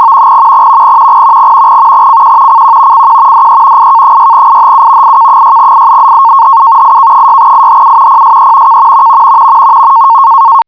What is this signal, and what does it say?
NAVTEX (SITOR-B), about 100 baud. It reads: CQ CQ CQ DE IK6IHU IK6IHU IK6IHU PSE K